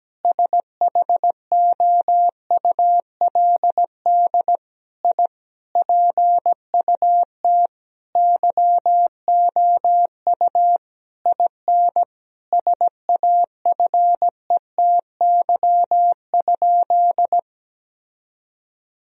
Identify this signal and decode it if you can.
Morse code, heard as a keyed tone, about 17 wpm, 700 Hz. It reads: SHOULD I PUT YOU IN SAFETY?
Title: Sentences from Top 1000 Words - Review Part 4 - 17wpm